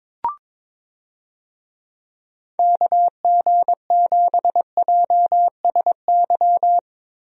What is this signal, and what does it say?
Morse code, heard as a keyed tone, about 22 wpm, 700 Hz. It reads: KG7JSY
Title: USA Call Signs - 6 Characters - Repeated 2x - 22wpm